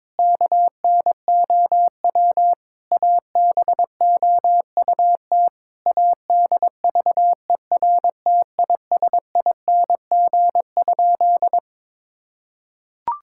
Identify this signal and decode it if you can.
Morse code, heard as a keyed tone, about 22 wpm, 700 Hz. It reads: KNOW ABOUT ADVERTISING?
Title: Sentences from Top 1000 Words - Review Part 6 - 22wpm